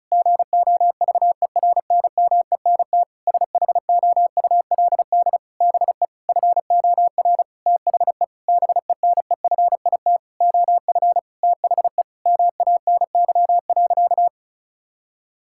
Morse code, heard as a keyed tone, about 36 wpm, 700 Hz. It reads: GOVERNMENT SHOULD BE FOR THE BENEFIT OF THE MANY.
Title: Sentences from Top 1000 Words - Review Part 6 - 35wpm